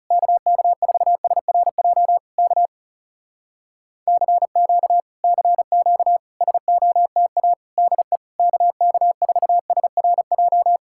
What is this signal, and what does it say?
Morse code, heard as a keyed tone, about 40 wpm, 700 Hz. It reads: KK4SRJ K CQ CQ SOTA DE KK4SRJ